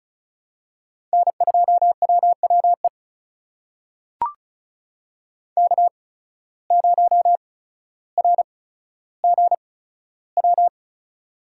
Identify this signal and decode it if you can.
Morse code, heard as a keyed tone, about 36 wpm, spaced as for 15 wpm, 690 Hz. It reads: N2WWE K0RGW
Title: USA Call Signs - 5 Characters - 8x Character Spacing - 35wpm